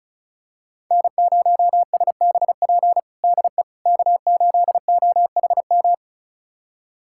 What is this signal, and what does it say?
Morse code, heard as a keyed tone, about 34 wpm, 700 Hz. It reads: N0SBP DE K8OHM
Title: USA Callsign DE Callsign - 5 Characters - 35wpm